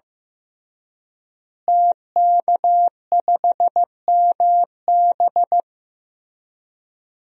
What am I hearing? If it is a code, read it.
Morse code, heard as a keyed tone, about 16 wpm, 700 Hz. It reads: TK5MB
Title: DXCC-100 Call Signs - 5 Characters - Repeated 3x - 15wpm